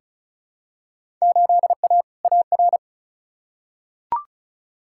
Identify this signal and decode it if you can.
Morse code, heard as a keyed tone, about 36 wpm, 700 Hz. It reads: 8A AR
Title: Field Day Exchanges Encore - 35wpm